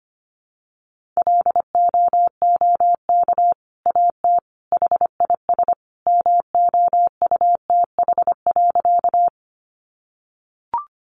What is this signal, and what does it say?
Morse code, heard as a keyed tone, about 25 wpm, 700 Hz. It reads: LOOK AT HIS MOUTH.
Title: Sentences from Top 1500 Words 25wpm